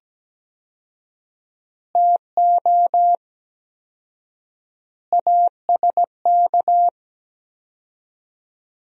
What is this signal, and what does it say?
Morse code, heard as a keyed tone, about 17 wpm, 700 Hz. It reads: TO ASK
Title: 4x Word-Spacing - Sentences from Top 200 Words - 17wpm